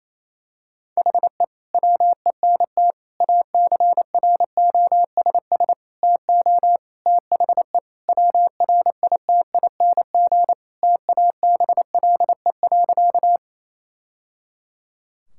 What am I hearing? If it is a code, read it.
Morse code, heard as a keyed tone, about 28 wpm, 700 Hz. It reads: HE WENT ACROSS TO THE WRITING TABLE.
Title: Sentences from Top 1000 Words - Review Part 5 - 28wpm